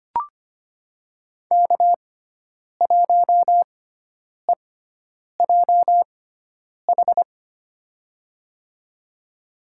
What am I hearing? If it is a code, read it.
Morse code, heard as a keyed tone, about 25 wpm, spaced as for 9 wpm, 700 Hz. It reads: K1EJH
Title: USA Call Signs - 6x Character Spacing - 25wpm